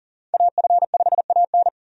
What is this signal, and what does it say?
Morse code, heard as a keyed tone, about 39 wpm, 700 Hz. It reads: AF5AN
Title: USA Call Signs - 5 Characters - 40wpm